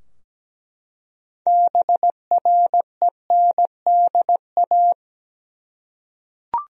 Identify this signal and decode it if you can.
Morse code, heard as a keyed tone, about 17 wpm, 700 Hz. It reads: BRENDA
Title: USA Common Names - 17wpm